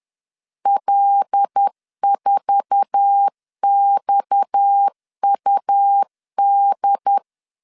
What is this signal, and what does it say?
Morse code, heard as a keyed tone, about 10 wpm, 800 Hz. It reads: L4XUD
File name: LEZIONI / 8 YJ45 / 8 YJ4 10wpm 2633